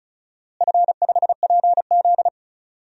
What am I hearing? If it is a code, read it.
Morse code, heard as a keyed tone, about 34 wpm, 700 Hz. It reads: F5PZ
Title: DXCC-100 Call Signs - 4 Characters - Repeated 3x - 35wpm